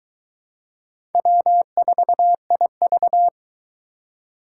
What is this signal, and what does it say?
Morse code, heard as a keyed tone, about 21 wpm, 700 Hz. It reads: W4IV